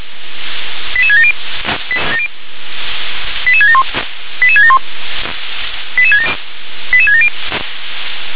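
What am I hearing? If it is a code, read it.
Five-tone selective call (ZVEI), heard as quick runs of five tones, tones of about 70 ms. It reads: 88670 88671 88671 88670
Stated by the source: ZVEI five-tone selective call received on SDR.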